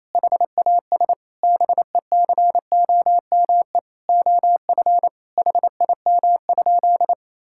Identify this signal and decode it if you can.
Morse code, heard as a keyed tone, about 28 wpm, 700 Hz. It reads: HAS BECOME OF HIM?
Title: Sentences from Top 1000 Words - Review Part 5 - 28wpm